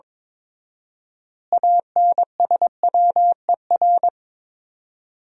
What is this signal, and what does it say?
Morse code, heard as a keyed tone, about 21 wpm, 700 Hz. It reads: ANSWER
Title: 4x Word-Spacing - Sentences from Top 300 Words - 22wpm